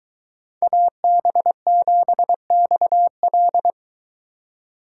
Morse code, heard as a keyed tone, about 22 wpm, 700 Hz. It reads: AB7XL